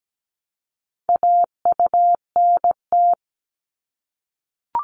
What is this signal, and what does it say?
Morse code, heard as a keyed tone, about 17 wpm, 700 Hz. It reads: AUNT
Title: Multi-letter Phonemes Repeated 3x 17wpm